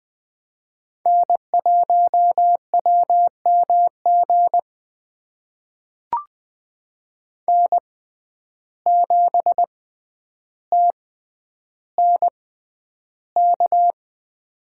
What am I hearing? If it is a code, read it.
Morse code, heard as a keyed tone, about 21 wpm, spaced as for 10 wpm, 700 Hz. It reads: N1WMG N7TNK
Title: USA Call Signs - 5 Characters - 6x Character Spacing - 20wpm